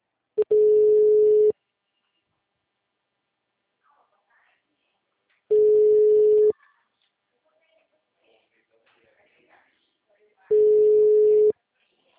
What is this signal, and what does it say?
Ringback tone (Europe, 425 Hz), with speech.
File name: out- -2014- - - .38694